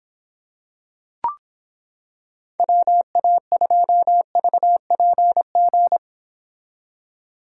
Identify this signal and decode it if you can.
Morse code, heard as a keyed tone, about 26 wpm, 700 Hz. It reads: WA2VPG